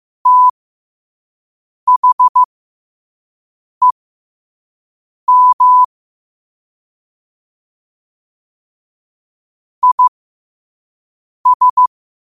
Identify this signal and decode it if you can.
Morse code, heard as a keyed tone, about 15 wpm, spaced as for 5 wpm, 1000 Hz. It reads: THEM IS
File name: qotdfiles / 5wpm